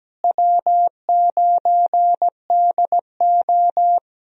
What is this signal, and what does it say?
Morse code, heard as a keyed tone, about 17 wpm, 700 Hz. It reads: W9DO
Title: USA Call Signs - 4 Characters - Repeated 3x - 17wpm